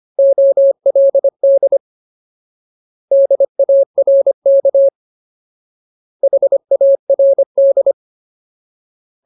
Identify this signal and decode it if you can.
Morse code, heard as a keyed tone, about 25 wpm, 550 Hz. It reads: OLD DARK HARD
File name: Ultimate Word List / 25WPM Ultimate